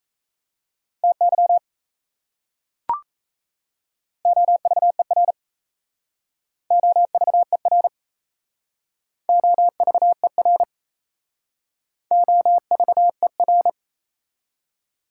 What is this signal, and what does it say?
Morse code, heard as a keyed tone, about 34 wpm, 700 Hz. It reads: TY OVER OVER OVER OVER
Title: Speed-Racing Prefixes and Suffixes 28wpm